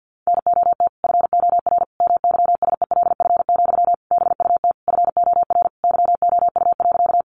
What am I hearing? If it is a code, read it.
Morse code, heard as a keyed tone, about 50 wpm, 700 Hz. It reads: NOT FOR MYSELF, BUT FOR YOU.
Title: Sentences from Top 1300 Words 50wpm